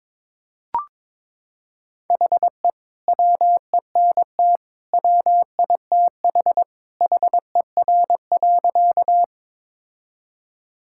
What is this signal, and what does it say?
Morse code, heard as a keyed tone, about 22 wpm, 700 Hz. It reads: HE WENT WITH HER.
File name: practice / 22wpm / Sentences with 2-Word Phrases - 22wpm